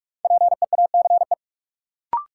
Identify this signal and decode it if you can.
Morse code, heard as a keyed tone, about 43 wpm, 700 Hz. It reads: PEACE
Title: Sentences from Top 1500 Words - New Word Drill - 45wpm